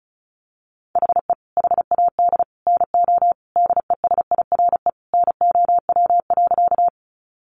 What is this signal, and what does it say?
Morse code, heard as a keyed tone, about 34 wpm, 700 Hz. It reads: HE HAD NO DESIRE NOW.
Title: Sentences from Top 1100 Words 35wpm